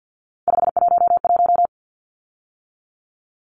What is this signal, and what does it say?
Morse code, heard as a keyed tone, about 50 wpm, 700 Hz. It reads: 511
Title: Numbers - 3-Digit 50wpm